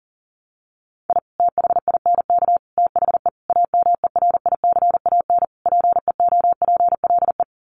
Morse code, heard as a keyed tone, about 37 wpm, 700 Hz. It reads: I THINK THE AMERICAN PEOPLE
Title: Sets of 5 Words Encore 40wpm